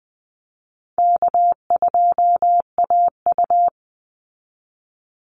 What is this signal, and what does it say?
Morse code, heard as a keyed tone, about 20 wpm, 700 Hz. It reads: K2AU
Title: Easy Call Signs Repeated 3x - 20wpm